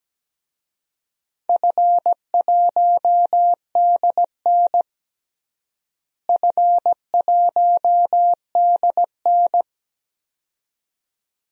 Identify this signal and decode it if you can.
Morse code, heard as a keyed tone, about 17 wpm, 700 Hz. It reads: F1DN F1DN
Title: DXCC-100 Call Signs - 4 Characters - Repeated 2x - 17wpm